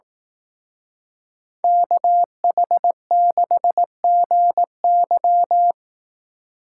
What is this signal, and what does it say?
Morse code, heard as a keyed tone, about 18 wpm, 700 Hz. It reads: KH6GY
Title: Speed Racing - USA 2x2 Call Signs 15wpm